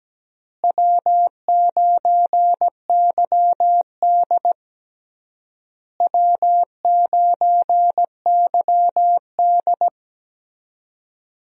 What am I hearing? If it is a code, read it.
Morse code, heard as a keyed tone, about 17 wpm, 700 Hz. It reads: W9YD W9YD
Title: USA Call Signs - 4 Characters - Repeated 3x - 17wpm